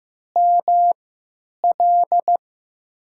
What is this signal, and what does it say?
Morse code, heard as a keyed tone, about 15 wpm, spaced as for 8 wpm, 700 Hz. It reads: ML
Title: USA Call Signs - 3x Character Spacing - 15wpm